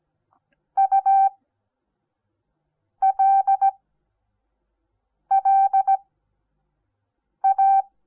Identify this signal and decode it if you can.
Morse code, heard as a keyed tone, about 17 wpm, spaced as for 5 wpm, 780 Hz. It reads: ULLA
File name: Onlinelehrgang / morsen / DJ4UF Morsen08